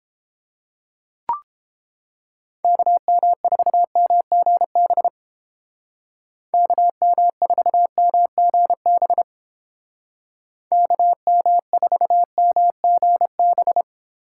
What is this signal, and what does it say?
Morse code, heard as a keyed tone, about 32 wpm, 700 Hz. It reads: KM4MGB KM4MGB KM4MGB